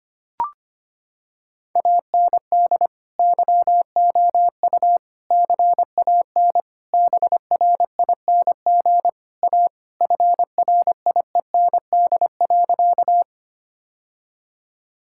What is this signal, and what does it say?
Morse code, heard as a keyed tone, about 25 wpm, 700 Hz. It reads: AND YOU CAN BRING A FRIEND.
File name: practice / 25wpm / Sentences with 2-Word Phrases - 25wpm